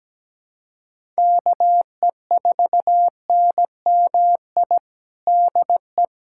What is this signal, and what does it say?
Morse code, heard as a keyed tone, about 17 wpm, 700 Hz. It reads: KE4NMI DE
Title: USA Callsign DE Callsign - 6 Characters - 17wpm